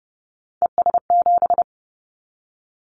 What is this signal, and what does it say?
Morse code, heard as a keyed tone, about 30 wpm, 700 Hz. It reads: ES7
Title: Callsign Prefixes for Top 100 DXCC - 30wpm